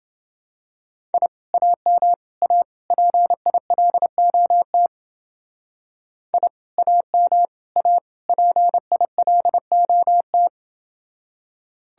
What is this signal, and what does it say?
Morse code, heard as a keyed tone, about 29 wpm, 700 Hz. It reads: I AM A PILOT I AM A PILOT